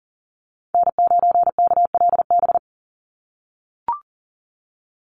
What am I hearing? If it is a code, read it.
Morse code, heard as a keyed tone, about 40 wpm, 700 Hz. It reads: N9KLB
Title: USA Call Signs 40wpm